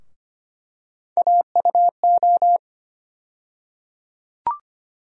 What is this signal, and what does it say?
Morse code, heard as a keyed tone, about 25 wpm, 700 Hz. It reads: AUO
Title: USA Call Sign Postfixes - 3 Letters 25wpm